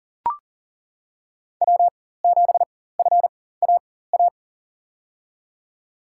Morse code, heard as a keyed tone, about 39 wpm, spaced as for 19 wpm, 700 Hz. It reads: W7FAA
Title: USA Call Signs - 5 Characters - 4x Character Spacing - 40wpm